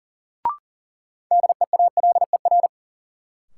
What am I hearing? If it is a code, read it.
Morse code, heard as a keyed tone, about 40 wpm, 700 Hz. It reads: DEALER
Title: Sentences from Top 1100 Words - New Word Drill Alt - 40wpm